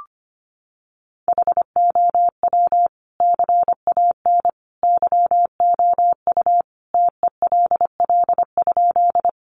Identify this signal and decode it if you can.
Morse code, heard as a keyed tone, about 25 wpm, 700 Hz. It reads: HOW CAN YOU TELL?
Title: Sentences from Top 200 Words 25wpm